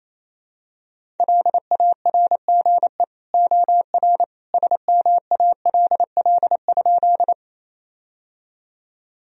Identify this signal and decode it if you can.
Morse code, heard as a keyed tone, about 28 wpm, 700 Hz. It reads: LARGE OR SMALL?
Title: Sentences from Top 1000 Words - Review Part 1 - 28wpm